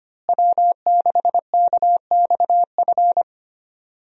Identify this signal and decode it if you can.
Morse code, heard as a keyed tone, about 25 wpm, 700 Hz. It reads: W6KXF
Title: USA Call Signs - 5 Characters - Repeated 2x - 25wpm